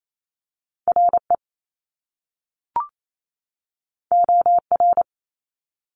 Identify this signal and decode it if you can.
Morse code, heard as a keyed tone, about 28 wpm, 700 Hz. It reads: RE OR
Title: Top Two Letter Combinations in Words 28wpm